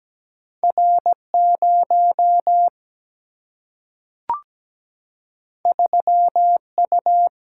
Morse code, heard as a keyed tone, about 17 wpm, 700 Hz. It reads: R0 3U
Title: Single-Letter Single-Number 17wpm